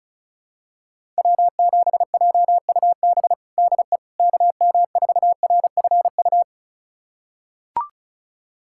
Morse code, heard as a keyed tone, about 36 wpm, 700 Hz. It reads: W7JUB DE KM4RFU
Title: USA Callsign DE Callsign 35wpm